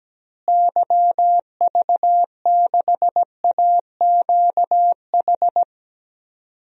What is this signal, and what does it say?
Morse code, heard as a keyed tone, about 17 wpm, 700 Hz. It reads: YV6AQH
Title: DXCC-100 Call Signs - 6 Characters - Repeated 2x - 17wpm